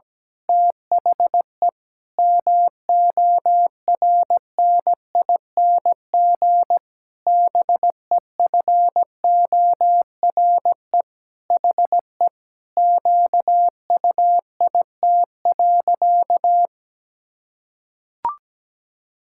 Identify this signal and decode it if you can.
Morse code, heard as a keyed tone, about 17 wpm, 700 Hz. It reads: THE MORNING BEFORE HE QUIT.
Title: Pangram Challenge v2 - 17wpm